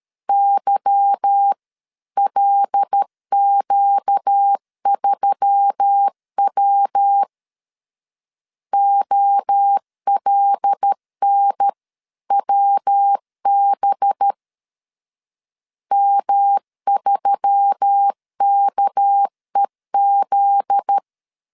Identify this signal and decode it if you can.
Morse code, heard as a keyed tone, about 13 wpm, 800 Hz. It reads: YLQ3W OLNWB M3KEZ
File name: LEZIONI / 8 YJ45 / 8 Y 12wpm 2228